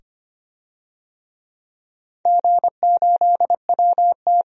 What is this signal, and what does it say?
Morse code, heard as a keyed tone, about 25 wpm, 700 Hz. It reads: G8WT
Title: DXCC-100 Call Signs - 4 Characters - 25wpm